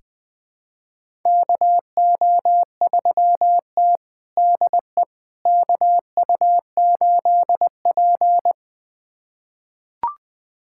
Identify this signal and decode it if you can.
Morse code, heard as a keyed tone, about 21 wpm, 700 Hz. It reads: KO3T DE KU8P
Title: USA Callsign DE Callsign - 4 Characters - 20wpm